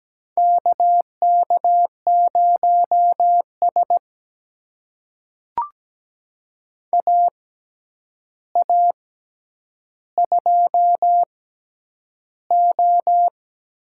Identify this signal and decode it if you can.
Morse code, heard as a keyed tone, about 17 wpm, spaced as for 9 wpm, 700 Hz. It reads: KK0S AA2O
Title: USA Call Signs - 4 Characters - 6x Character Spacing - 17wpm